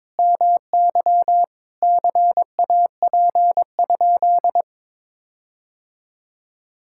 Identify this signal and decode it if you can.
Morse code, heard as a keyed tone, about 21 wpm, 700 Hz. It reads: MY CAP?
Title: Sentences from Top 1600 Words 22wpm